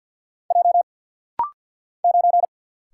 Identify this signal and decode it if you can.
Morse code, heard as a keyed tone, about 50 wpm, 690 Hz. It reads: J 9
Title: Single Letter-Number Alt - Rapid-Fire 50wpm